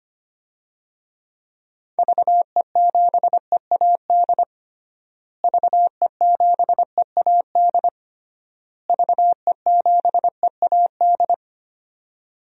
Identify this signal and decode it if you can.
Morse code, heard as a keyed tone, about 25 wpm, 700 Hz. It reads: VE7EAD VE7EAD VE7EAD
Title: DXCC-100 Call Signs - 6 Characters - Repeated 3x - 25wpm